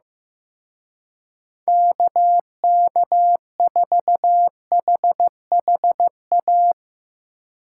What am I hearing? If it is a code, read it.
Morse code, heard as a keyed tone, about 15 wpm, 700 Hz. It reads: KK4HHA
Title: Speed Racing - USA 2x3 Call Signs 15wpm